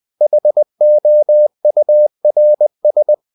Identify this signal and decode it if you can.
Morse code, heard as a keyed tone, about 20 wpm, 600 Hz. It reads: HOURS
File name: cw-books / winnie-the-pooh / winnie-20wpm-page-0024